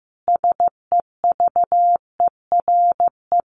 Morse code, heard as a keyed tone, about 14 wpm, 700 Hz. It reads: SEVERE
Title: Sentences from Top 1400 Words 15wpm